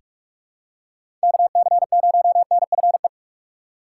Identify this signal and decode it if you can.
Morse code, heard as a keyed tone, about 47 wpm, 700 Hz. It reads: KC0NFE
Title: USA Call Signs - 6 Characters - 4x Character Spacing - 45wpm